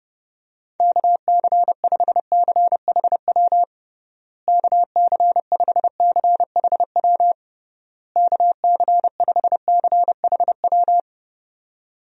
Morse code, heard as a keyed tone, about 30 wpm, 700 Hz. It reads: KC5CHW KC5CHW KC5CHW
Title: USA Call Signs - 6 Characters - Repeated 3x - 30wpm